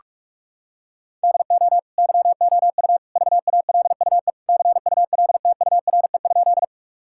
Morse code, heard as a keyed tone, about 46 wpm, 700 Hz. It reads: DO YOU VALUE CULTURE?
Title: Sentences from Top 1000 Words - Review Part 6 - 45wpm